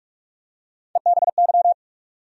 Morse code, heard as a keyed tone, about 46 wpm, 700 Hz. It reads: EBY
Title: USA Call Sign Postfixes - 3 Letters 45wpm